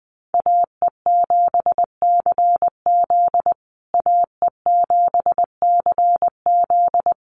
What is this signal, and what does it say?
Morse code, heard as a keyed tone, about 19 wpm, 700 Hz. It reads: AE7CZ AE7CZ
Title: USA Call Signs Repeated 3x 20wpm